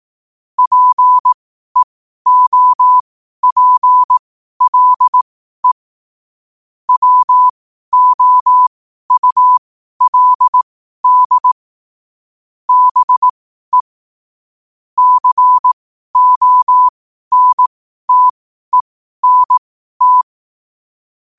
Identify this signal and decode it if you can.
Morse code, heard as a keyed tone, about 18 wpm, spaced as for 12 wpm, 1000 Hz. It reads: PEOPLE WOULD BE CONTENT
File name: qotdfiles / 12wpm